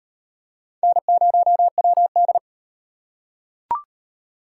Morse code, heard as a keyed tone, about 34 wpm, 700 Hz. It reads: N0WD